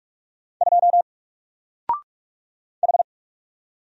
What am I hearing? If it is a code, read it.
Morse code, heard as a keyed tone, about 44 wpm, 700 Hz. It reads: 2 H
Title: Single Character (Letter:Number:Punctuation:Prosign) 45wpm